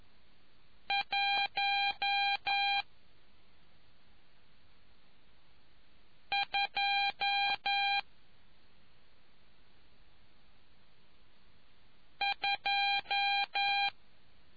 Morse code, heard as a keyed tone, about 11 wpm, spaced as for 2 wpm, 800 Hz. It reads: 122